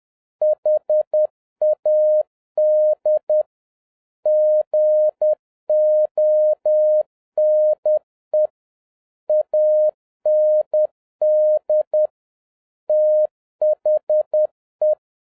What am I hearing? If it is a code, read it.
Morse code, heard as a keyed tone, about 10 wpm, 610 Hz. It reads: HAD GONE AND THE
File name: 2011-07-mp3 / n31 THE 3.11 2 10wpm